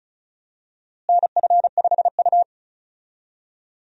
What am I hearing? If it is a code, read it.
Morse code, heard as a keyed tone, about 34 wpm, 700 Hz. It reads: NF5U
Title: USA Call Signs - 4 Characters - Repeated 3x - 35wpm